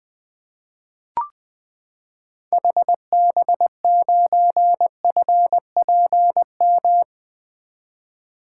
Morse code, heard as a keyed tone, about 20 wpm, 700 Hz. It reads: HB9FPM
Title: 2023 SOTA Activator Call Signs - 20wpm